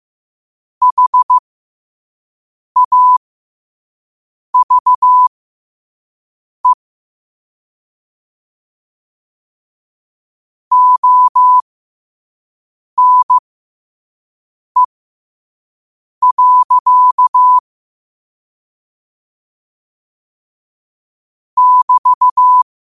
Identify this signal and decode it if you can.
Morse code, heard as a keyed tone, about 15 wpm, spaced as for 5 wpm, 1000 Hz. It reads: HAVE ONE. <BT>
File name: 5wpm